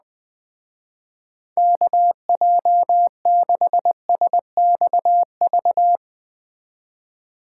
Morse code, heard as a keyed tone, about 20 wpm, 700 Hz. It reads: KJ6SXV